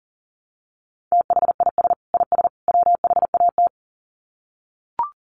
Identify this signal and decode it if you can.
Morse code, heard as a keyed tone, about 40 wpm, 700 Hz. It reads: THIS IS WHAT
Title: Sets of 3 Words 40wpm